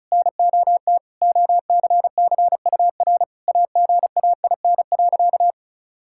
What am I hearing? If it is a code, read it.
Morse code, heard as a keyed tone, about 36 wpm, 700 Hz. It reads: NOT OCCUR AGAIN.
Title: Sentences from Top 1000 Words - Review Part 6 - 35wpm